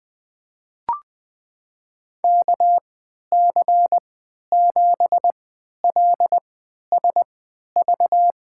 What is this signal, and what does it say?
Morse code, heard as a keyed tone, about 20 wpm, spaced as for 11 wpm, 700 Hz. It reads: KC7LSV
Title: USA Call Signs - 6 Characters - 3x Character Spacing - 20wpm